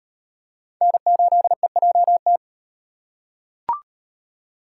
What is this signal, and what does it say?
Morse code, heard as a keyed tone, about 38 wpm, 700 Hz. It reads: N8EJT